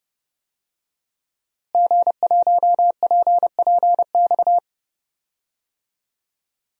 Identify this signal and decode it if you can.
Morse code, heard as a keyed tone, about 30 wpm, 700 Hz. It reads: G1PPX
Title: DXCC-100 Call Signs - 5 Characters - Repeated 2x - 30wpm